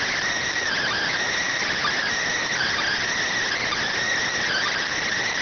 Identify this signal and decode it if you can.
Emergency Alert System (SAME) tones, heard as harsh buzzing bursts. It is heard from 0.7 to 4.9 s.
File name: findings / MR AXIAL 33,99mm 1.gen-44 rx